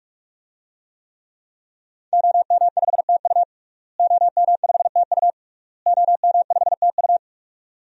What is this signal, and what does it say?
Morse code, heard as a keyed tone, about 47 wpm, 700 Hz. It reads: OM5TU OM5TU OM5TU
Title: DXCC-100 Call Signs - 5 Characters - Repeated 3x - 45wpm